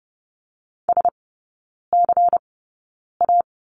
Morse code, heard as a keyed tone, about 30 wpm, spaced as for 9 wpm, 700 Hz. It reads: SCA
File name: Lesson 18e - Letter m - Three Characters 30wpm